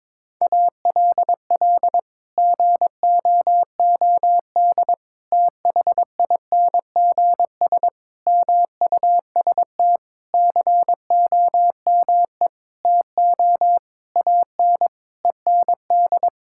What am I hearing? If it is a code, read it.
Morse code, heard as a keyed tone, about 22 wpm, 700 Hz. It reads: ALL GOOD THINGS MUST COME TO AN END
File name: Sets of 9 Words - 22wpm